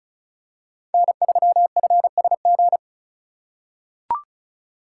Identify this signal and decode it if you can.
Morse code, heard as a keyed tone, about 34 wpm, 700 Hz. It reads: N3FSG
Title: USA Call Signs - 5 Characters - 3x Character Spacing - 35wpm